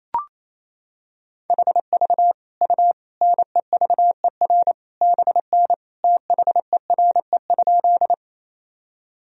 Morse code, heard as a keyed tone, about 28 wpm, 700 Hz. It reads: HV U NEVER BN THERE?
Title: Sentences from Top 300 Words and Abbreviations 28wpm